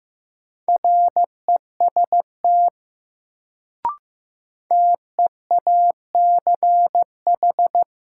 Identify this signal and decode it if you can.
Morse code, heard as a keyed tone, about 15 wpm, 700 Hz. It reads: REST TEACH
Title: Sentences from Top 500 Words - New Word Drill - 15wpm